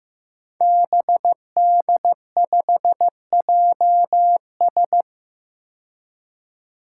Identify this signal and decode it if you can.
Morse code, heard as a keyed tone, about 15 wpm, 700 Hz. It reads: BD5JS
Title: Five Challenging Letter-Number Combinations 15wpm